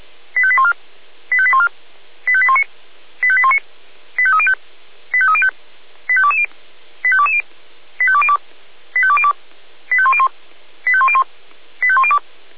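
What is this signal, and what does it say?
Five-tone selective call (ZVEI), heard as quick runs of five tones, tones of about 70 ms. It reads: 85814 85814 85819 85819 85305 85305 85220 85220 85202 85202 85101 85101 85102
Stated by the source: ZVEI five-tone call received by SDR, fire brigade Orlamünde (feuerwehr-orlamuende) recording.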